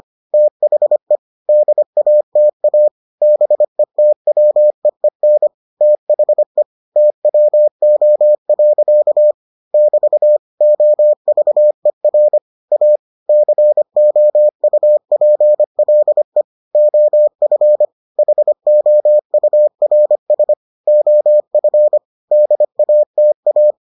Morse code, heard as a keyed tone, about 25 wpm, 600 Hz. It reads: THE DATA BETWEEN THE TWO. <BT> OVER A COUPLE OF HOURS OF DATA